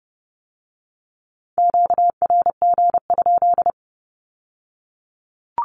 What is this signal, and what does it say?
Morse code, heard as a keyed tone, about 30 wpm, 700 Hz. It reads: QRG?
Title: Q-Codes with Spoken Meaning - 30wpm